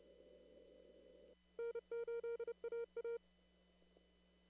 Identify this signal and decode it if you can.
Morse code, heard as a keyed tone, about 30 wpm, 470 Hz. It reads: N8AA
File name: CW / kt5j wpx cw-op-k5pi